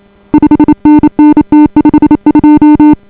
Morse code, heard as a keyed tone, about 28 wpm, 280 Hz. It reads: 5NN T52